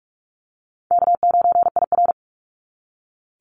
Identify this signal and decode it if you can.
Morse code, heard as a keyed tone, about 46 wpm, 700 Hz. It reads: K9IR
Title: NA SOTA Chaser Callsigns 45wpm